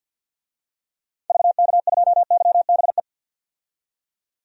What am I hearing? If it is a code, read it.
Morse code, heard as a keyed tone, about 47 wpm, 700 Hz. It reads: VK2YBE
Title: DXCC-100 Call Signs - 6 Characters - 50wpm